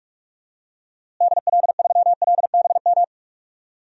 Morse code, heard as a keyed tone, about 42 wpm, 700 Hz. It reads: DL3LBM
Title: DXCC-100 Call Signs - 6 Characters - Repeated 3x - 45wpm